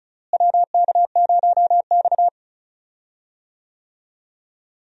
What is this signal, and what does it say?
Morse code, heard as a keyed tone, about 34 wpm, 700 Hz. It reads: WK0X
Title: USA Call Signs - 4 Characters - Repeated 2x - 35wpm